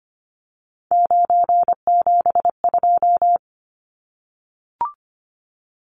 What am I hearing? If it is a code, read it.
Morse code, heard as a keyed tone, about 25 wpm, 690 Hz. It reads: 972
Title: Numbers - 3-Digit 25wpm